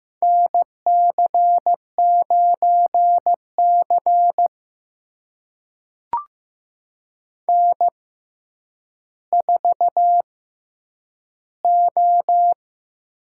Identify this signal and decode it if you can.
Morse code, heard as a keyed tone, about 15 wpm, spaced as for 9 wpm, 700 Hz. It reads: NC9C N4O
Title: USA Call Signs - 4 Characters - 6x Character Spacing - 15wpm